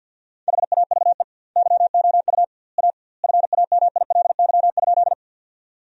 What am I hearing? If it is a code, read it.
Morse code, heard as a keyed tone, about 50 wpm, 700 Hz. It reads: HAVE YOU A FAMILY?
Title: 2x Word-Spacing - Sentences from Top 400 Words - 50wpm